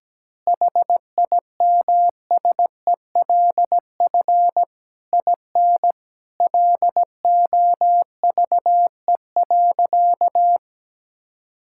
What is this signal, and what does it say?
Morse code, heard as a keyed tone, about 17 wpm, 700 Hz. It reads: HIMSELF IN LOVE.
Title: Sentences from Top 1600 Words 17wpm